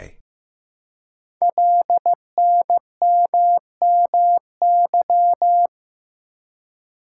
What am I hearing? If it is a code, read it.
Morse code, heard as a keyed tone, about 15 wpm, 700 Hz. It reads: LNMMY
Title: Code Groups - 5 Letters - 15wpm